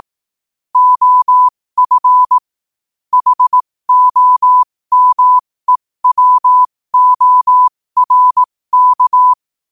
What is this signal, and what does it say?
Morse code, heard as a keyed tone, about 18 wpm, spaced as for 15 wpm, 1000 Hz. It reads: OF HOMEWORK
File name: qotdfiles / 15wpm